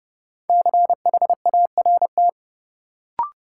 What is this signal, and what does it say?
Morse code, heard as a keyed tone, about 30 wpm, 700 Hz. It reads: CHART
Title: Sentences from Top 1300 Words - New Word Drill - 30wpm